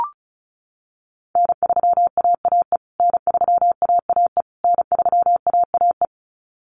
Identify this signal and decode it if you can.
Morse code, heard as a keyed tone, about 34 wpm, 700 Hz. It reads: N3AAE N3AAE N3AAE
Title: USA Call Signs Repeated 3x 35wpm